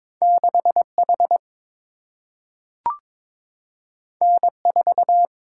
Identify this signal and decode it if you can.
Morse code, heard as a keyed tone, about 22 wpm, 700 Hz. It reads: 6H N4
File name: practice / 22wpm / Two Letter-Number Combinations - 22wpm